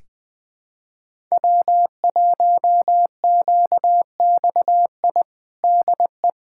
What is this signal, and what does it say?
Morse code, heard as a keyed tone, about 20 wpm, 700 Hz. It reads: W1QXI DE
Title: USA Callsign DE Callsign - 5 Characters - 20wpm